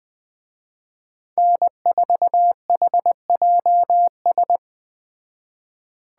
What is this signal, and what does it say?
Morse code, heard as a keyed tone, about 20 wpm, 700 Hz. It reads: N4HJS
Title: Speed Racing - USA 1x3 Call Signs 17wpm